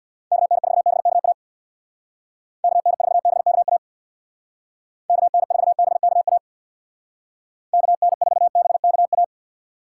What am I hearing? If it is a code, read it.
Morse code, heard as a keyed tone, about 59 wpm, 700 Hz. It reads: KN4BKA KN4BKA KN4BKA KN4BKA
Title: Speed Racing - USA 2x3 Call Signs 50wpm